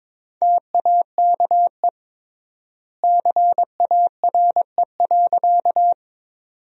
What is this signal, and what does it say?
Morse code, heard as a keyed tone, about 22 wpm, 700 Hz. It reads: TAKE CARE.
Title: 3x Word-Spacing - Sentences from Top 300 Words - 22wpm